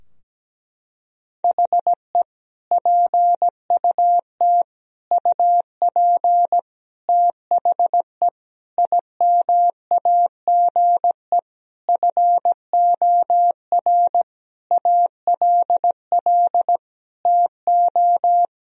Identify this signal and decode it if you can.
Morse code, heard as a keyed tone, about 17 wpm, 700 Hz. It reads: HE PUT UP THE IMAGE FOR ALL TO
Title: Sentences from Top 1000 Words - Review Part 4 - 17wpm